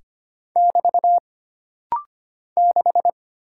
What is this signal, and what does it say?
Morse code, heard as a keyed tone, about 25 wpm, 700 Hz. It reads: <BT> 6
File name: Single Character (Letter:Number:Punctuation:Prosign) - Rapid-Fire - 25wpm